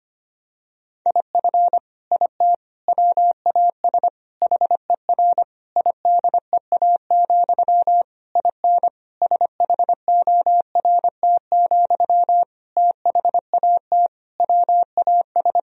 Morse code, heard as a keyed tone, about 25 wpm, 700 Hz. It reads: IF IT WAS HER IDEA, IN SHORT, THAT WAS
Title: Sentences from Top 1000 Words - Review Part 2 - 25wpm